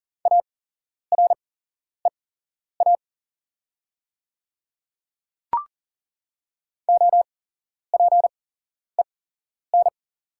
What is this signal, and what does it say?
Morse code, heard as a keyed tone, about 40 wpm, spaced as for 11 wpm, 700 Hz. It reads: AREA OPEN
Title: 8x Character Spacing - Top 500 Words - 40wpm